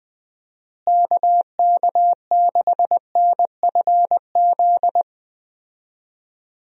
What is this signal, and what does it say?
Morse code, heard as a keyed tone, about 20 wpm, 700 Hz. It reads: KK6NFZ
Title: USA Call Signs - 6 Characters - Repeated 2x - 20wpm